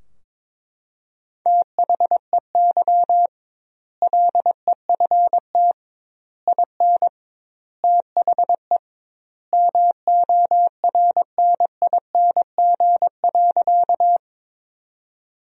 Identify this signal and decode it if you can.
Morse code, heard as a keyed tone, about 22 wpm, 700 Hz. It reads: THEY LEFT IN THE MORNING.
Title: 2x Word-Spacing - Sentences from Top 400 Words - 22wpm